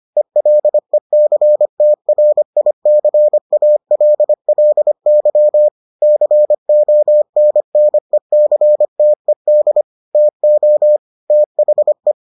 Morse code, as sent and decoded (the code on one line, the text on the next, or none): . .-.. -.-. - .-. .. -.-. .- .-.. .-.. -.-- / -.-. --- -. -. -.-. - . -.. / - --- / - .... .
ELECTRICALLY CONNECTED TO THE